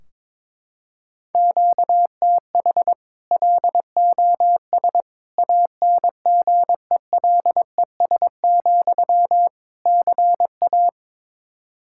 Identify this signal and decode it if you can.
Morse code, heard as a keyed tone, about 22 wpm, 700 Hz. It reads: QTH LOS ANGELES, CA
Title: Practice QSO - Exchange 1 - Beginning - 22wpm